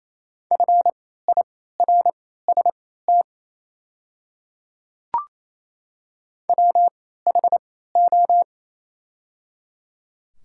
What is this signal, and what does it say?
Morse code, heard as a keyed tone, about 28 wpm, spaced as for 16 wpm, 700 Hz. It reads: FIRST WHO
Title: 3x Character Spacing - Top 100 Words - 28wpm